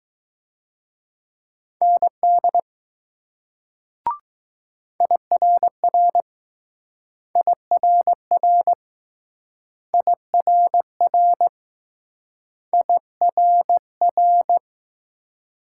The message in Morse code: ND IRR IRR IRR IRR

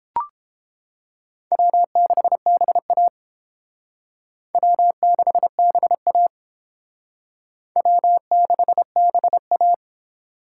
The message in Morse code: W6BA W6BA W6BA

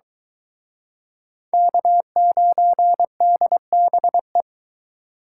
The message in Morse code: K9DBE

